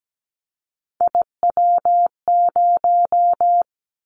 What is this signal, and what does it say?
Morse code, heard as a keyed tone, about 17 wpm, 700 Hz. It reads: IW0
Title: Callsign Prefixes for Top 25 DXCC - 17wpm